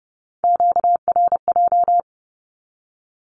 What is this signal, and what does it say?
Morse code, heard as a keyed tone, about 30 wpm, 700 Hz. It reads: QRJ
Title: Q-Codes 30wpm